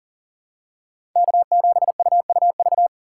Morse code, heard as a keyed tone, about 40 wpm, 700 Hz. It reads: K7UUV
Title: USA Call Signs - 5 Characters - 4x Character Spacing - 40wpm